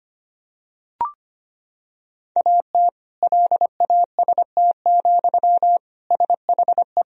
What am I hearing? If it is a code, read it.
Morse code, heard as a keyed tone, about 25 wpm, 700 Hz. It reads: AT LAST, SHE
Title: Sentences from Top 1000 Words - Review Part 2 - 25wpm